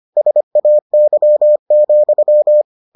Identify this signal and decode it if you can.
Morse code, heard as a keyed tone, about 25 wpm, 600 Hz. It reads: SAY,